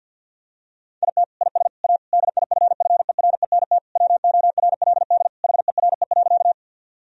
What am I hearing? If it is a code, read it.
Morse code, heard as a keyed tone, about 50 wpm, 700 Hz. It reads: IT IS A DIFFERENT WORLD HERE.
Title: Sentences from Top 1000 Words - Review Part 3 - 50wpm